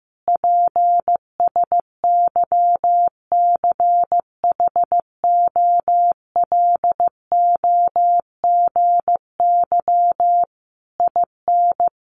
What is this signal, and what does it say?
Morse code, heard as a keyed tone, about 15 wpm, 700 Hz. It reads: PSYCHOLOGY IN
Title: Sentences from Top 1400 Words 15wpm